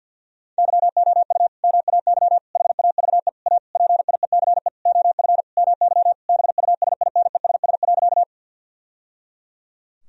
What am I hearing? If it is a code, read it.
Morse code, heard as a keyed tone, about 50 wpm, 700 Hz. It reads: YOU MAY HAVE A PIECE OF MY BUSINESS.
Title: Sentences from Top 1000 Words - Review Part 2 - 50wpm